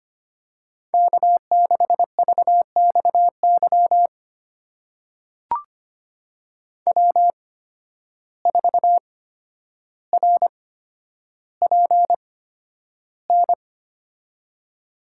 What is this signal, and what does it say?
Morse code, heard as a keyed tone, about 25 wpm, spaced as for 11 wpm, 700 Hz. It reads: K6VXY W4RPN
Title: USA Call Signs - 5 Characters - 8x Character Spacing - 25wpm